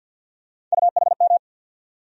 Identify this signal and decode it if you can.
Morse code, heard as a keyed tone, about 53 wpm, 700 Hz. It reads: UHM
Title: USA Call Sign Postfixes - 3 Letters 50wpm